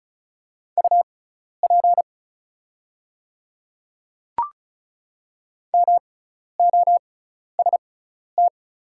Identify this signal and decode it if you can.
Morse code, heard as a keyed tone, about 36 wpm, spaced as for 12 wpm, 700 Hz. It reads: UP MOST
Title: 6x Character Spacing - Top 100 Words - 35wpm